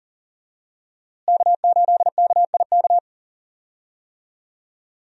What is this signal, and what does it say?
Morse code, heard as a keyed tone, about 40 wpm, 700 Hz. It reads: K8KIK
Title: USA Call Signs - 5 Characters - Repeated 2x - 40wpm